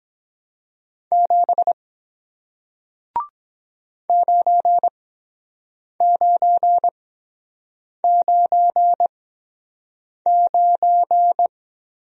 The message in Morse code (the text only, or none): 7 9999